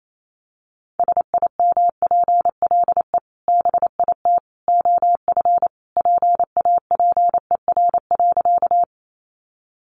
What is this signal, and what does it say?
Morse code, heard as a keyed tone, about 28 wpm, 690 Hz. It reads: SIMPLE BIT OF PAPER.
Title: Sentences from Top 900 Words 28wpm